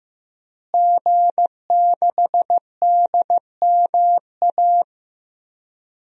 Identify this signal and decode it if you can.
Morse code, heard as a keyed tone, about 15 wpm, 700 Hz. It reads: G6DMA
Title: DXCC-100 Call Signs - 5 Characters - Repeated 3x - 15wpm